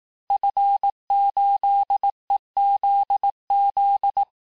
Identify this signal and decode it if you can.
Morse code, heard as a keyed tone, about 18 wpm, 800 Hz. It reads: F8EZZ
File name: cours-du-lundi files / cours-1588-FIN-de-QSO-vitesse-17MPM